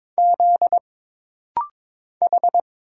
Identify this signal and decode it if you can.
Morse code, heard as a keyed tone, about 22 wpm, 700 Hz. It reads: Z H
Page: This rapid-fire timing is geared to push your ICR (Instant Character Recognition) skills.